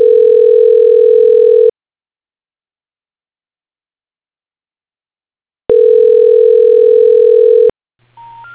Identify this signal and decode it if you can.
Ringback tone (North America), 440 + 480 Hz.